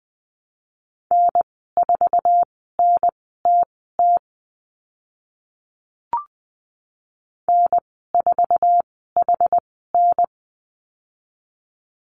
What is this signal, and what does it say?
Morse code, heard as a keyed tone, about 20 wpm, spaced as for 14 wpm, 700 Hz. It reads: N4NTT N4HN
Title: 062 Lesson 11g - Letter h - Call Signs 20wpm